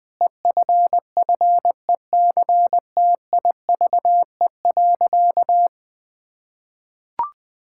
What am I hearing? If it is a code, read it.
Morse code, heard as a keyed tone, about 20 wpm, 700 Hz. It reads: EFFECTIVE.
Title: Sentences from Top 1000 Words - Review Part 6 - 20wpm